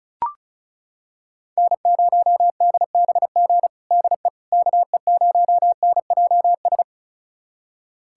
Morse code, heard as a keyed tone, about 36 wpm, 700 Hz. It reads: N0DBG DE KE0NJS
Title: USA Callsign DE Callsign 35wpm